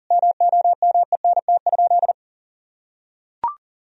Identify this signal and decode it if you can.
Morse code, heard as a keyed tone, about 40 wpm, 700 Hz. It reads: MOMENT?
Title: Sentences from Top 1000 Words - Review Part 1 - 40wpm